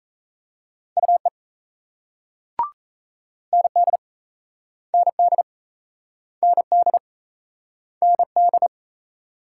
Morse code, heard as a keyed tone, about 38 wpm, 700 Hz. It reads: UE ND ND ND ND